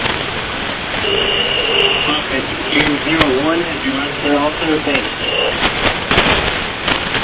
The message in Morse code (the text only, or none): none